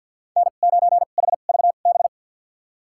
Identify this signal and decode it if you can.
Morse code, heard as a keyed tone, about 52 wpm, spaced as for 36 wpm, 700 Hz. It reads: N9HVB
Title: USA Call Signs - 5 Characters - 2x Character Spacing - 50wpm